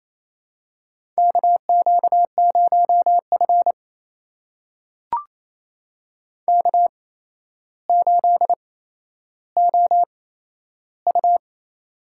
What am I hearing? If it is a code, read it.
Morse code, heard as a keyed tone, about 28 wpm, spaced as for 12 wpm, 700 Hz. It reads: KQ0F K8OU
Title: USA Call Signs - 4 Characters - 8x Character Spacing - 28wpm